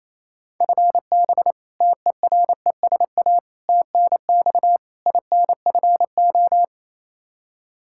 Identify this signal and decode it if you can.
Morse code, heard as a keyed tone, about 28 wpm, 700 Hz. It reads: FB TERESA TNX INFO
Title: Practice QSO - Exchange 3 - More Info - 28wpm